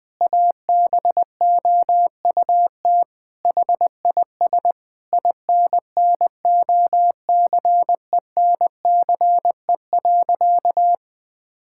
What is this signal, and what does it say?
Morse code, heard as a keyed tone, about 20 wpm, 700 Hz. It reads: ABOUT HIS INNOCENCE.
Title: Sentences with 7-Word Phrases - 20wpm